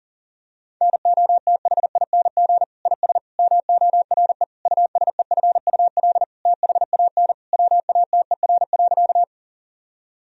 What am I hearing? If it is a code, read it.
Morse code, heard as a keyed tone, about 40 wpm, 700 Hz. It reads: NOTHING IS MORE USEFUL THAN WATER.
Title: Sentences from Top 1000 Words - Review Part 4 - 40wpm